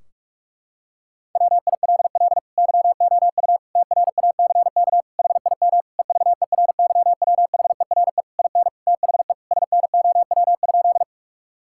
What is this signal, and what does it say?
Morse code, heard as a keyed tone, about 46 wpm, 700 Hz. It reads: WILL YOU TRACK HIM EVERYWHERE IN THE SNOW?
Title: Sentences from Top 1600 Words 45wpm